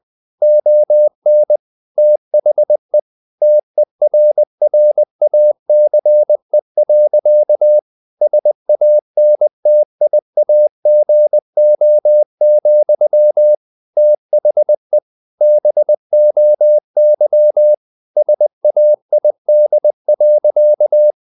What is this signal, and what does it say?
Morse code, heard as a keyed tone, about 20 wpm, 600 Hz. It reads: ON THE TERRACE. SANTIAGO, THE BOY SAID.
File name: the old man and the sea / the old man and the sea part 1 20wpm